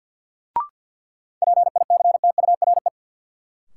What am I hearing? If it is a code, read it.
Morse code, heard as a keyed tone, about 53 wpm, 700 Hz. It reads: PICTURE
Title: Sentences from Top 500 Words - New Word Drill - 50wpm